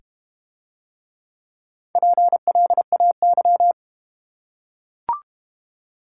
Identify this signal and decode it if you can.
Morse code, heard as a keyed tone, about 31 wpm, 700 Hz. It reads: PLAY
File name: Easy Words - Speed-Racing with Hints - 25wpm